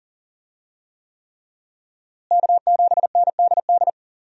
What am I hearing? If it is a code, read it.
Morse code, heard as a keyed tone, about 40 wpm, 700 Hz. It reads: K7NDD
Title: USA Call Signs - 5 Characters - Repeated 3x - 40wpm